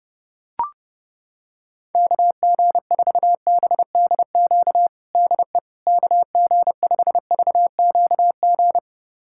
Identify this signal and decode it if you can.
Morse code, heard as a keyed tone, about 30 wpm, 700 Hz. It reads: KG4BDQ DE KG5VQG
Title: USA Callsign DE Callsign 30wpm